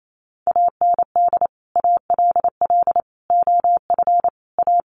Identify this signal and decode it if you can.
Morse code, heard as a keyed tone, about 28 wpm, 700 Hz. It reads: AND ALL OF A
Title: Sets of 5 Words 28wpm